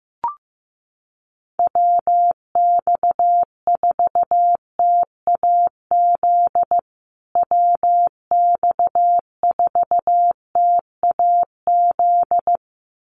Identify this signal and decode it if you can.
Morse code, heard as a keyed tone, about 15 wpm, 700 Hz. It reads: WX4TAZ WX4TAZ
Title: USA Call Signs Repeated 2x 15wpm